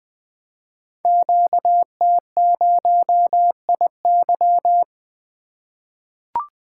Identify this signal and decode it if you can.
Morse code, heard as a keyed tone, about 20 wpm, 700 Hz. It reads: QT0IY
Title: Code Groups - 5 Letter:Numbers - 20wpm